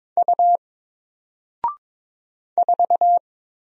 Morse code, heard as a keyed tone, about 22 wpm, 700 Hz. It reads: U 4
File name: Single-Letter-Number-22wpm